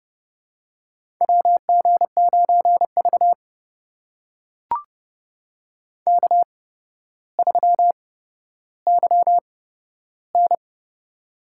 Morse code, heard as a keyed tone, about 30 wpm, spaced as for 13 wpm, 700 Hz. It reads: WG9V K3YN
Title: USA Call Signs - 4 Characters - 8x Character Spacing - 30wpm